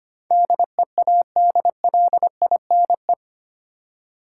Morse code, heard as a keyed tone, about 25 wpm, 700 Hz. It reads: DEADLINE
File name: 8 Letter Words - First 3 Letters Given - 25wpm